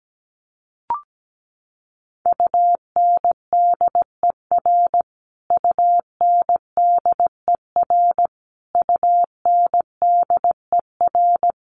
Morse code, heard as a keyed tone, about 17 wpm, 700 Hz. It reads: UNDER UNDER UNDER
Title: Top 200 Words Repeated 17wpm